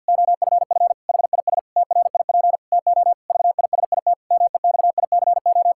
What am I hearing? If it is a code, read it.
Morse code, heard as a keyed tone, about 50 wpm, 700 Hz. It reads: OFF HIS TRIP TO VISIT MEXICO
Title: Pangram Challenge v2 - 50wpm